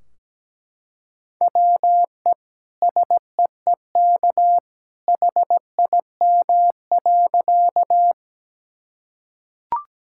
Morse code, heard as a keyed tone, about 17 wpm, 700 Hz. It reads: WE SEEK HIM.
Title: Sentences from Top 1600 Words 17wpm